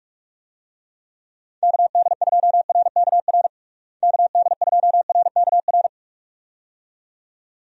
Morse code, heard as a keyed tone, about 41 wpm, 700 Hz. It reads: KD2RKR KD2RKR
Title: USA Call Signs - 6 Characters - Repeated 2x - 45wpm